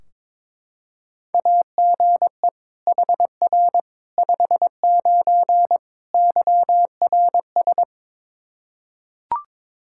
Morse code, heard as a keyed tone, about 22 wpm, 700 Hz. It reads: AGE HR 59 YRS
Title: Practice QSO - Exchange 3 - More Info - 22wpm